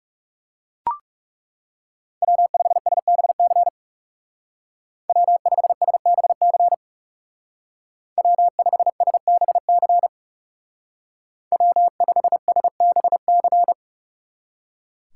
Morse code, heard as a keyed tone, about 35 wpm, 700 Hz. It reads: W5SBC W5SBC W5SBC W5SBC
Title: Speed Racing - USA 1x3 Call Signs 30wpm